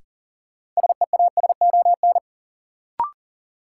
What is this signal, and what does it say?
Morse code, heard as a keyed tone, about 37 wpm, 700 Hz. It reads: SEASON
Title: Sentences from Top 1400 Words - New Word Drill Alt - 40wpm